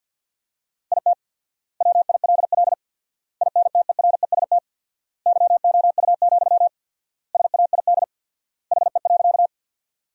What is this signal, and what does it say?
Morse code, heard as a keyed tone, about 53 wpm, 700 Hz. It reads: IT WILL INTEREST YOU, SAID HE.
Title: 4x Word-Spacing - Sentences from Top 400 Words - 50wpm